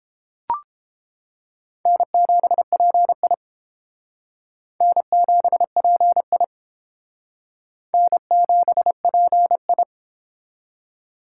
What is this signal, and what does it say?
Morse code, heard as a keyed tone, about 30 wpm, 700 Hz. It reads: N7PI N7PI N7PI